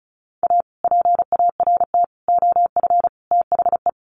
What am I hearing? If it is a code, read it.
Morse code, heard as a keyed tone, about 34 wpm, 700 Hz. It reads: A PART OF THE
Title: Sets of 4 Words 35wpm